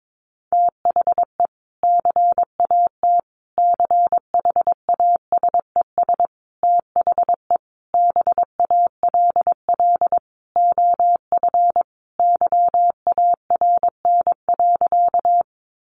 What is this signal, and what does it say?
Morse code, heard as a keyed tone, about 22 wpm, 700 Hz. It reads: THE CAT CHASES THE BALL OF YARN.
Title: Additional Easy Phrases 22wpm